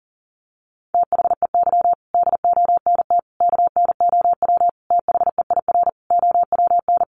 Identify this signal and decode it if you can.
Morse code, heard as a keyed tone, about 40 wpm, 700 Hz. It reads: THEY DONT KNOW THEIR OWN
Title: Sentences from Top 300 Words 40wpm